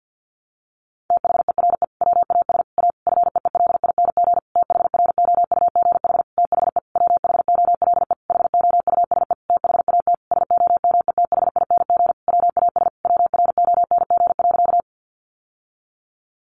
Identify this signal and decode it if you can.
Morse code, heard as a keyed tone, about 50 wpm, 700 Hz. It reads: THERE WAS A FEELING THROUGH THE WHOLE HOUSE THAT SOMETHING WAS WRONG.
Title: Sentences from Top 1200 Words 50wpm